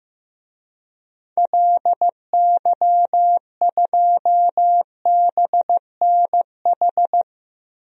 Morse code, heard as a keyed tone, about 15 wpm, 700 Hz. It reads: LY2BNH
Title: DXCC-100 Call Signs - 6 Characters - Repeated 3x - 15wpm